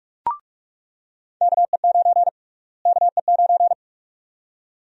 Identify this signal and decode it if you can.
Morse code, heard as a keyed tone, about 47 wpm, 700 Hz. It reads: KE9 KE9
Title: USA Call Sign Prefixes - Repeated 2x - 45wpm